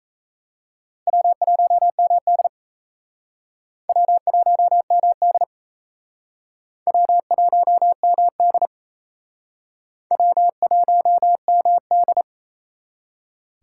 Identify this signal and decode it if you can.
Morse code, heard as a keyed tone, about 36 wpm, 700 Hz. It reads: W1MD W1MD W1MD W1MD